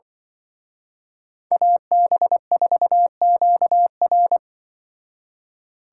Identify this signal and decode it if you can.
Morse code, heard as a keyed tone, about 25 wpm, 700 Hz. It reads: AB4QR
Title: Speed Racing - USA Call Signs 20wpm